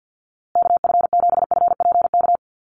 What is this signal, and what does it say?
Morse code, heard as a keyed tone, about 49 wpm, 700 Hz. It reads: KF7FPK
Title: USA Call Signs Repeated 2x 50wpm